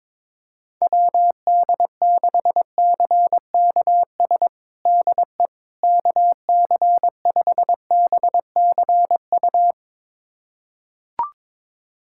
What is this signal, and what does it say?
Morse code, heard as a keyed tone, about 22 wpm, 700 Hz. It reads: WD6CKS DE KC5BCU
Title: USA Callsign DE Callsign - 6 Characters - 22wpm